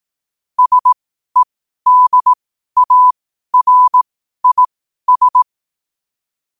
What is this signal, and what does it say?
Morse code, heard as a keyed tone, about 18 wpm, spaced as for 12 wpm, 1000 Hz. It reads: SEDARIS
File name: qotdfiles / 12wpm